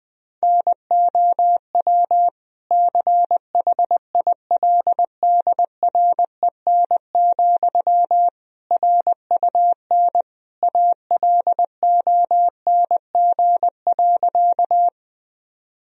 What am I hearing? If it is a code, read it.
Morse code, heard as a keyed tone, about 20 wpm, 700 Hz. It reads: NOW CHILDREN, RUN ALONG.
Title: Sentences from Top 1000 Words - Review Part 1 - 20wpm